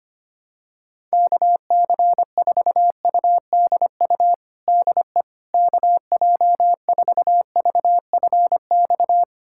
Morse code, heard as a keyed tone, about 25 wpm, 700 Hz. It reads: KC4UDU DE KJ4VFX
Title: USA Callsign DE Callsign - 6 Characters - 25wpm